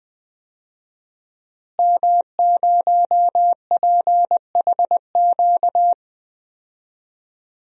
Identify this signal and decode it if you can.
Morse code, heard as a keyed tone, about 20 wpm, 700 Hz. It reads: M0PHQ
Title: DXCC-100 Call Signs - 5 Characters - Repeated 3x - 20wpm